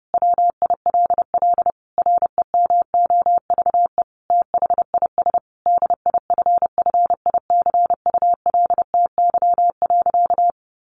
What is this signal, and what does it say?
Morse code, heard as a keyed tone, about 30 wpm, 700 Hz. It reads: WILL REMOVE THIS DIFFICULTY.
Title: Sentences from Top 1500 Words 30wpm